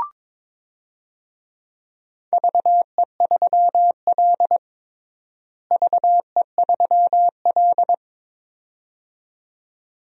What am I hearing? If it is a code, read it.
Morse code, heard as a keyed tone, about 22 wpm, 700 Hz. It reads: VE3L VE3L
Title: DXCC-100 Call Signs - 4 Characters - Repeated 2x - 22wpm